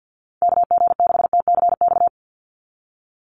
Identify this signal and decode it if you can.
Morse code, heard as a keyed tone, about 48 wpm, 700 Hz. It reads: KG6TCX
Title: USA Call Signs 50wpm